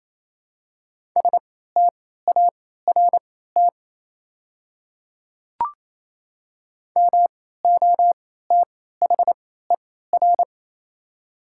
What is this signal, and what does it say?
Morse code, heard as a keyed tone, about 27 wpm, spaced as for 16 wpm, 700 Hz. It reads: START MOTHER
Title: 3x Character Spacing - Top 300 Words - 28wpm